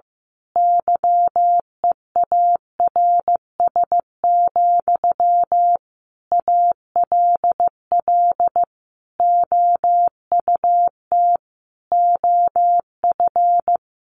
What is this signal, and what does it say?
Morse code, heard as a keyed tone, about 15 wpm, 700 Hz. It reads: YEARS, ALL OUT OF
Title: A Sound Of Thunder 15wpm